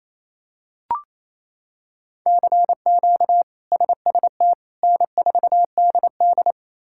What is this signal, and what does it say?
Morse code, heard as a keyed tone, about 28 wpm, 700 Hz. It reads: CQ SST N4DD
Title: K1USN SST QSO Practice - 28wpm